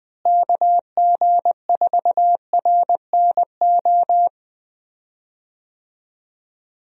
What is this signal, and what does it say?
Morse code, heard as a keyed tone, about 21 wpm, 700 Hz. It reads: KG4RNO
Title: USA Call Signs - 6 Characters - Repeated 3x - 20wpm